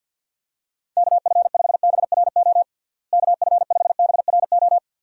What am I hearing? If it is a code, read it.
Morse code, heard as a keyed tone, about 46 wpm, 700 Hz. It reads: KF5BRO KF5BRO
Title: USA Call Signs - 6 Characters - Repeated 3x - 50wpm